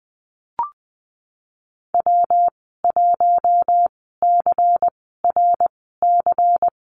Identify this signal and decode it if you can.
Morse code, heard as a keyed tone, about 19 wpm, spaced as for 14 wpm, 700 Hz. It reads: W1CRC
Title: 122 Lesson 20g - Number 3 - Call Signs 20wpm